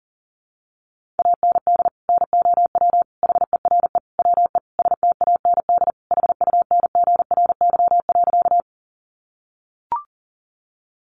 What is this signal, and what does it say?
Morse code, heard as a keyed tone, about 40 wpm, 700 Hz. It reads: AND NOW HERE WE STAND HUNGRY.
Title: Sentences from Top 1400 Words 40wpm